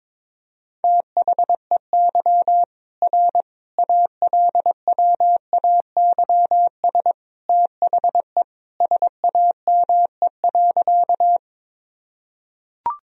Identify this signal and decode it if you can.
Morse code, heard as a keyed tone, about 22 wpm, 700 Hz. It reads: THEY R ALWAYS THE SAME.
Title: Sentences from Top 300 Words and Abbreviations 22wpm